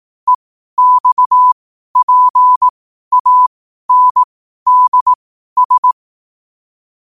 Morse code, heard as a keyed tone, about 18 wpm, spaced as for 12 wpm, 1000 Hz. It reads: EXPANDS
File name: qotdfiles / 12wpm